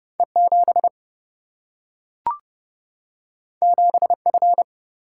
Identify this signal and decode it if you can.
Morse code, heard as a keyed tone, about 30 wpm, 700 Hz. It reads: E7 7F
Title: Single-Letter Single-Number 30wpm